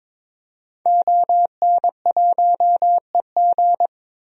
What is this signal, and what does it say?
Morse code, heard as a keyed tone, about 22 wpm, 700 Hz. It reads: ON1EG
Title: DXCC-100 Call Signs - 5 Characters - 22wpm